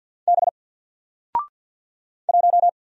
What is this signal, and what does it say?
Morse code, heard as a keyed tone, about 48 wpm, 700 Hz. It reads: B 1
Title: Single Character (Letter:Number:Punctuation:Prosign) Alt 50wpm